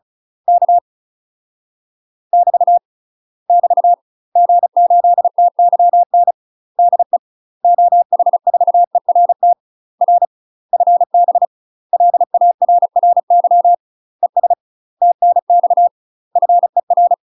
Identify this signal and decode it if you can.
Morse code, heard as a keyed tone, about 34 wpm, 700 Hz. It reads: K <BT> <BT> G8TYN DE OH4ERT R FB LARRY ES TNX FER
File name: UK MASTER QSOs TRAINING / 35WPM / UK MASTER QSOs 0043